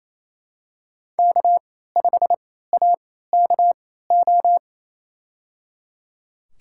Morse code, heard as a keyed tone, about 28 wpm, spaced as for 16 wpm, 700 Hz. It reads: K5AKO
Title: USA Call Signs - 3x Character Spacing - 28wpm